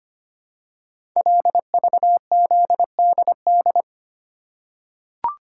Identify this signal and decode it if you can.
Morse code, heard as a keyed tone, about 25 wpm, 700 Hz. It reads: LVZDD
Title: Code Groups - 5 Letters - 25wpm